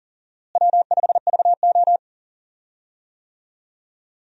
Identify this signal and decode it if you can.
Morse code, heard as a keyed tone, about 40 wpm, 700 Hz. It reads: W5VO
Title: USA Call Signs - 4 Characters - Repeated 2x - 40wpm